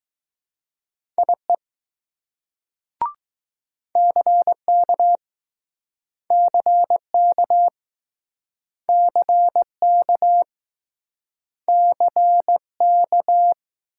Morse code, heard as a keyed tone, about 20 wpm, 700 Hz. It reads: IE CK CK CK CK